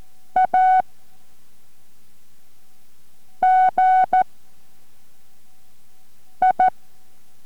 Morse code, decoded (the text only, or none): AGI